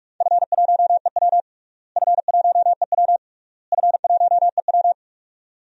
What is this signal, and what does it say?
Morse code, heard as a keyed tone, about 42 wpm, 700 Hz. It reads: F1EW F1EW F1EW
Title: DXCC-100 Call Signs - 4 Characters - Repeated 3x - 45wpm